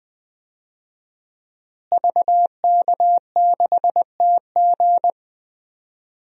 Morse code, heard as a keyed tone, about 20 wpm, 700 Hz. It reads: VK6TG
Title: DXCC-100 Call Signs - 5 Characters - Repeated 3x - 20wpm